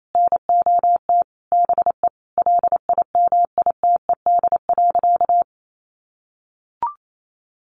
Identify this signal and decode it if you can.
Morse code, heard as a keyed tone, about 28 wpm, 700 Hz. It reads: NOT BE LIMITED.
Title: Sentences from Top 900 Words 28wpm